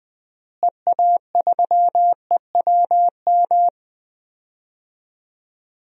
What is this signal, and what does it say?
Morse code, heard as a keyed tone, about 20 wpm, 700 Hz. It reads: EA3EWM
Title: DXCC-100 Call Signs - 6 Characters - Repeated 3x - 20wpm